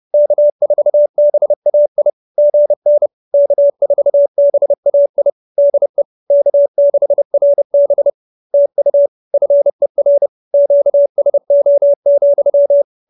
Morse code, heard as a keyed tone, about 30 wpm, 590 Hz. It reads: K4BAI GN K4BAI DE K6RB TU FER QSO,